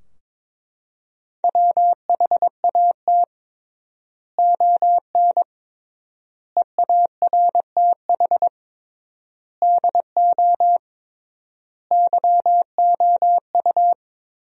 Morse code, heard as a keyed tone, about 22 wpm, 700 Hz. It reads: WHAT ON EARTH DO YOU
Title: 3x Word-Spacing - Sentences from Top 200 Words - 22wpm